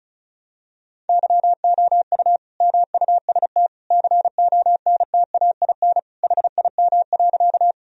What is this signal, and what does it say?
Morse code, heard as a keyed tone, about 34 wpm, 700 Hz. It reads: YOU MUST CONTAIN HIM.
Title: Sentences from Top 1000 Words - Review Part 2 - 35wpm